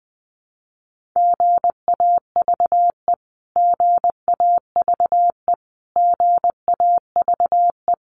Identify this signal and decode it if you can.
Morse code, heard as a keyed tone, about 20 wpm, 700 Hz. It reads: GAVE GAVE GAVE
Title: Top 500 Words Repeated 20wpm